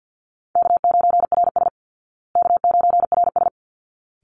Morse code, heard as a keyed tone, about 50 wpm, 700 Hz. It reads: K9RS K9RS
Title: Easy Call Signs Repeated 2x - 50wpm